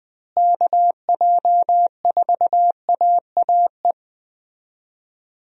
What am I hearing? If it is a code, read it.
Morse code, heard as a keyed tone, about 20 wpm, 700 Hz. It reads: KJ4AAE
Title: USA Call Signs - 6 Characters - Repeated 2x - 20wpm